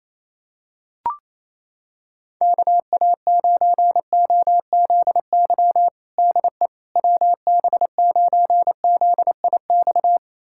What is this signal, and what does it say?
Morse code, heard as a keyed tone, about 28 wpm, 700 Hz. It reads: KA9OZY DE WB9ZIX
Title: USA Callsign DE Callsign - 6 Characters - 28wpm